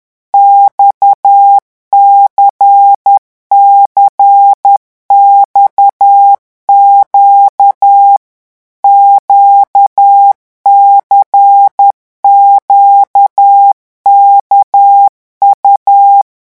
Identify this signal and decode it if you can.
Morse code, heard as a keyed tone, about 10 wpm, 800 Hz. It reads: XCCXQ QCQKU
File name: morse mp3 / revision2-vitesse 10 mots